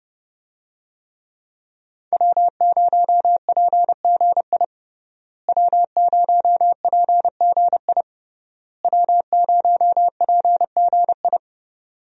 Morse code, heard as a keyed tone, about 30 wpm, 700 Hz. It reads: W0PGI W0PGI W0PGI
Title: USA Call Signs - 5 Characters - Repeated 3x - 30wpm